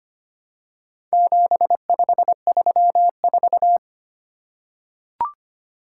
Morse code, heard as a keyed tone, about 25 wpm, 700 Hz. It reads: 7534
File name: practice / 25wpm / Four Letter-Number Combinations - 25wpm